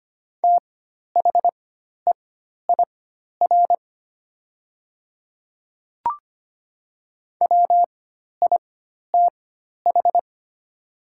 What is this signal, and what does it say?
Morse code, heard as a keyed tone, about 25 wpm, spaced as for 12 wpm, 700 Hz. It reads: THEIR WITH
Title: 4x Character Spacing - Top 100 Words - 25wpm